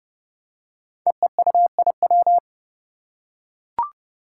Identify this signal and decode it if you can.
Morse code, heard as a keyed tone, about 30 wpm, 700 Hz. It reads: EEUIW
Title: Five Challenging Letter-Number Combinations 30wpm